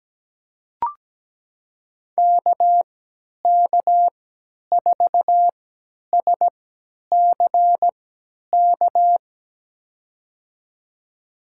Morse code, heard as a keyed tone, about 17 wpm, spaced as for 10 wpm, 700 Hz. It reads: KK4SCK
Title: USA Call Signs - 6 Characters - 3x Character Spacing - 17wpm